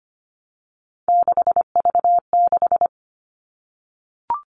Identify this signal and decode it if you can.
Morse code, heard as a keyed tone, about 25 wpm, 700 Hz. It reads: 6V6
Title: All Cut Numbers - 3-Digit 25wpm